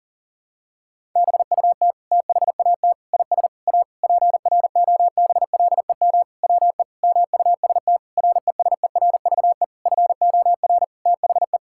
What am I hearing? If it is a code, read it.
Morse code, heard as a keyed tone, about 40 wpm, 700 Hz. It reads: BUT THAT IS A PROBLEM WE MUST RESERVE FOR THE
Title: Sentences from Top 1000 Words - Review Part 6 - 40wpm